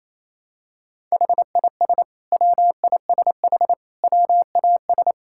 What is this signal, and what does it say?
Morse code, heard as a keyed tone, about 28 wpm, 700 Hz. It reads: HIS WISH WAS
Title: Sentences from Top 1600 Words 28wpm